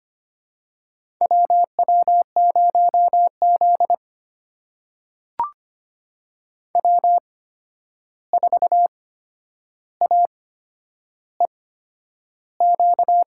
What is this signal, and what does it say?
Morse code, heard as a keyed tone, about 25 wpm, spaced as for 10 wpm, 700 Hz. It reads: WW0Z W4AEQ
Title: USA Call Signs - 8x Character Spacing - 25wpm